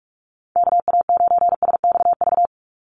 Morse code, heard as a keyed tone, about 44 wpm, 700 Hz. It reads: KA9SXV
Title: Call Signs Repeated 3x - 45wpm